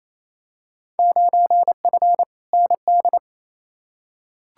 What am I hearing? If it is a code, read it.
Morse code, heard as a keyed tone, about 28 wpm, 700 Hz. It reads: 9F ND
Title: Field Day Exchanges Encore - 28wpm